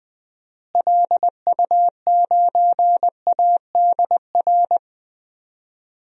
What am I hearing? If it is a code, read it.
Morse code, heard as a keyed tone, about 20 wpm, 700 Hz. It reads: LU9ADR
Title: DXCC-100 Call Signs - 6 Characters - Repeated 2x - 20wpm